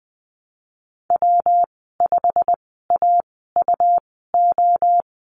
Morse code, heard as a keyed tone, about 20 wpm, spaced as for 15 wpm, 700 Hz. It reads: W5AUO
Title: 186 Lesson 30g - Letter b - Call Signs 20wpm